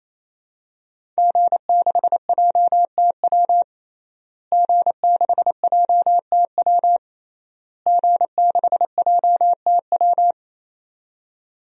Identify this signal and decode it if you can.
Morse code, heard as a keyed tone, about 28 wpm, 700 Hz. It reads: G6JTW G6JTW G6JTW
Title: DXCC-100 Call Signs - 5 Characters - Repeated 3x - 28wpm